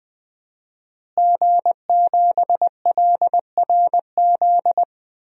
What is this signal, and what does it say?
Morse code, heard as a keyed tone, about 20 wpm, 700 Hz. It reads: G7LRZ
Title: DXCC-100 Call Signs - 5 Characters - Repeated 2x - 20wpm